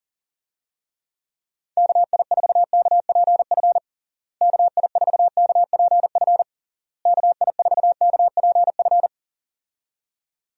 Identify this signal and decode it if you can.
Morse code, heard as a keyed tone, about 40 wpm, 700 Hz. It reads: KI4KPF KI4KPF KI4KPF
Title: USA Call Signs - 6 Characters - Repeated 3x - 40wpm